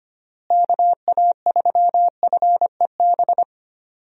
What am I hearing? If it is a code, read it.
Morse code, heard as a keyed tone, about 25 wpm, 700 Hz. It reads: KA3FEB
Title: USA Call Signs - 6 Characters - Repeated 3x - 25wpm